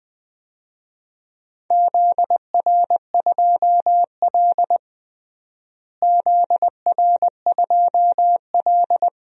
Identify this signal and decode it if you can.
Morse code, heard as a keyed tone, about 20 wpm, 700 Hz. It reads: ZR2L ZR2L
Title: DXCC-100 Call Signs - 4 Characters - Repeated 2x - 20wpm